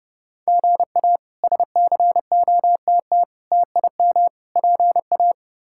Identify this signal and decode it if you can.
Morse code, heard as a keyed tone, about 29 wpm, 700 Hz. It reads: GA SCOTT TIM PA
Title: K1USN SST QSO Practice - 30wpm